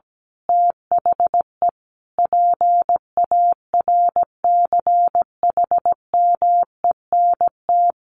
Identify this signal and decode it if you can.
Morse code, heard as a keyed tone, about 17 wpm, 700 Hz. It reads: THE PARCHMENT
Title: A Sound Of Thunder 17wpm